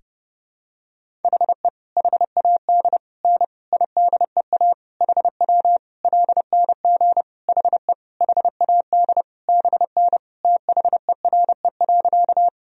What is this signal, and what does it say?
Morse code, heard as a keyed tone, about 30 wpm, 700 Hz. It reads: HE HAD N IDEA HW LNG HE HAD BN THERE.
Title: Sentences from Top 300 Words and Abbreviations 30wpm